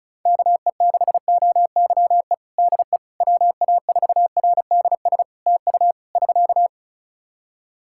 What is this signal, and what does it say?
Morse code, heard as a keyed tone, about 34 wpm, 700 Hz. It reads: KE6OYE DE WA4RDS TU <SK>
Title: Practice QSO - Exchange 4 - Ending - 35wpm